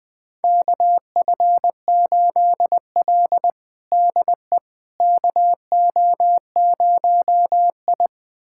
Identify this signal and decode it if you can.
Morse code, heard as a keyed tone, about 20 wpm, 700 Hz. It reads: KF8L DE KO0I
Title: USA Callsign DE Callsign - 4 Characters - 20wpm